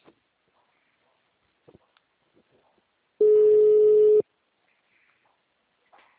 Ringback tone (Europe, 425 Hz).
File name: out- -2010- - - .1568